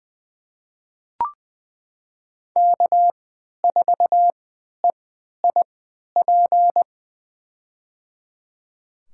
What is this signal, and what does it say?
Morse code, heard as a keyed tone, about 21 wpm, spaced as for 11 wpm, 700 Hz. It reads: K4EIP
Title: USA Call Signs - 5 Characters - 3x Character Spacing - 20wpm